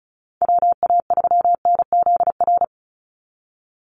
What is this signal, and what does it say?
Morse code, heard as a keyed tone, about 34 wpm, 700 Hz. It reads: WA3NZR
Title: Call Signs - 35wpm